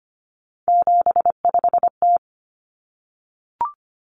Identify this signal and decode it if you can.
Morse code, heard as a keyed tone, about 25 wpm, 700 Hz. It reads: 75T
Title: Common Cut Numbers - 3-Digit 25wpm